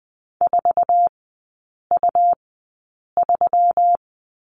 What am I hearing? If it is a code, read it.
Morse code, heard as a keyed tone, about 20 wpm, spaced as for 8 wpm, 700 Hz. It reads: 4U3
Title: 138 Lesson 23e - Letter f - Three Characters 20wpm